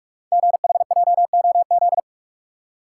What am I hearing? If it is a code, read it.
Morse code, heard as a keyed tone, about 44 wpm, 700 Hz. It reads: GHJOZ
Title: Code Groups - 5 Letters - 45wpm